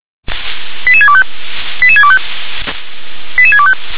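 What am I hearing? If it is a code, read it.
Five-tone selective call (ZVEI), heard as quick runs of five tones, tones of about 70 ms. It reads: 88625 88625 88625
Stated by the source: ZVEI five-tone selective call, received on an SDR.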